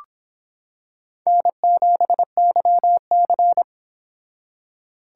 Morse code, heard as a keyed tone, about 26 wpm, 700 Hz. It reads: N7YC